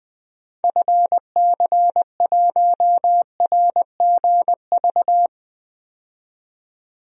Morse code, heard as a keyed tone, about 20 wpm, 700 Hz. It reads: FC1RGV
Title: DXCC-100 Call Signs - 6 Characters - 20wpm